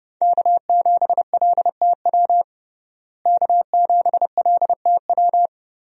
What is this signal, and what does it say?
Morse code, heard as a keyed tone, about 30 wpm, 700 Hz. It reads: K7LTW K7LTW
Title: USA Call Signs - 5 Characters - Repeated 2x - 30wpm